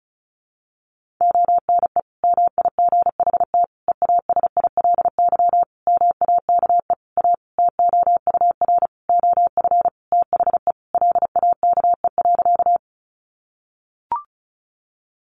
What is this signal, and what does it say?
Morse code, heard as a keyed tone, about 34 wpm, 700 Hz. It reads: ONE MIGHT EASILY MAKE A TOUR OF THE LAKE.
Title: Sentences from Top 1400 Words 35wpm